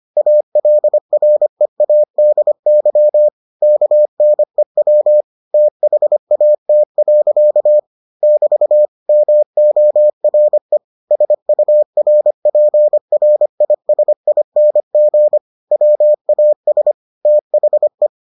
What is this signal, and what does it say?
Morse code, heard as a keyed tone, about 25 wpm, 600 Hz. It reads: ALREADY KNEW THAT. <BT> MORE SURPRISING WAS THE